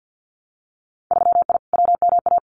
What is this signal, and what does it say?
Morse code, heard as a keyed tone, about 48 wpm, 700 Hz. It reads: 3I WMA
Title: Winter Field Day Exchanges 50wpm